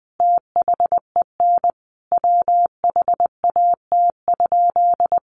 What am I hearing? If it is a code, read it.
Morse code, heard as a keyed tone, about 19 wpm, 700 Hz. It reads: THEN WHAT?
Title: Sentences from Top 1500 Words 20wpm